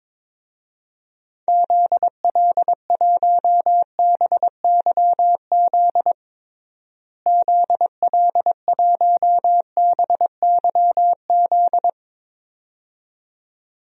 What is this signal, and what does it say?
Morse code, heard as a keyed tone, about 21 wpm, 700 Hz. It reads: ZL1BYZ ZL1BYZ
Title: DXCC-100 Call Signs - 6 Characters - Repeated 2x - 22wpm